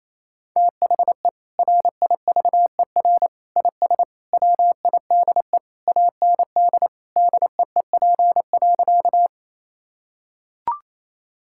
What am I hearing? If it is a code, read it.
Morse code, heard as a keyed tone, about 28 wpm, 700 Hz. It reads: THE RIVER IS WIDE AND DEEP.
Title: Sentences from Top 1000 Words - Review Part 6 - 28wpm